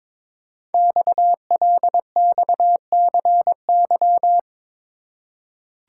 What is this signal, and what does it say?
Morse code, heard as a keyed tone, about 22 wpm, 700 Hz. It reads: XLXCY
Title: Code Groups - 5 Letters - 22wpm